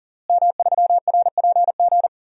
Morse code, heard as a keyed tone, about 40 wpm, 700 Hz. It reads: M3RPG
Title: DXCC-100 Call Signs - 5 Characters - Repeated 2x - 40wpm